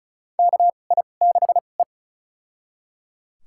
Morse code, heard as a keyed tone, about 34 wpm, spaced as for 25 wpm, 700 Hz. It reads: KI6E
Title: USA Call Signs - 4 Characters - 2x Character Spacing - 35wpm